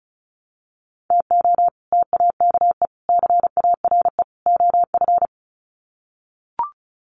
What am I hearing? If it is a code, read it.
Morse code, heard as a keyed tone, about 34 wpm, 700 Hz. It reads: TO TAKE CARE OF
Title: Sets of 4 Words 35wpm